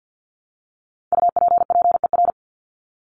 Morse code, heard as a keyed tone, about 50 wpm, 700 Hz. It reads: UPPER
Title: Top 1751-2000 Words 50wpm